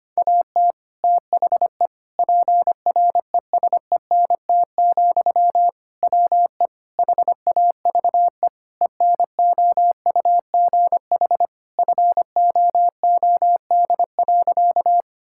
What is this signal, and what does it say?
Morse code, heard as a keyed tone, about 25 wpm, 700 Hz. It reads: AT THE PRESENT, WE HAVE ENOUGH FOOD.
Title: Sentences from Top 1000 Words - Review Part 5 - 25wpm